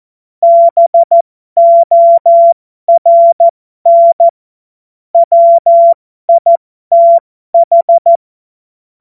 Morse code, heard as a keyed tone, about 14 wpm, spaced as for 12 wpm, 680 Hz. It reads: BORN WITH
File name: Morsekurs / Witze